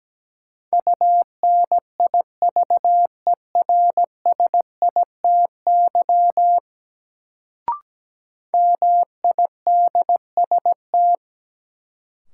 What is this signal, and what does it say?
Morse code, heard as a keyed tone, about 17 wpm, 700 Hz. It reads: UNIVERSITY MIDST
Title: Sets of 4 Words - New Word Drill - 17wpm